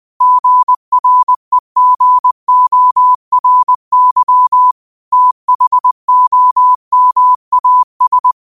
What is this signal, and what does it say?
Morse code, heard as a keyed tone, about 20 wpm, 1000 Hz. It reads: GREGORY THOMAS
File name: qotdfiles / 20wpm